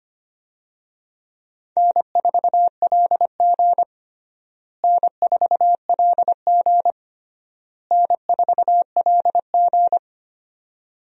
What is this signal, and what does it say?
Morse code, heard as a keyed tone, about 25 wpm, 700 Hz. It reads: N4LG N4LG N4LG
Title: USA Call Signs - 4 Characters - Repeated 3x - 25wpm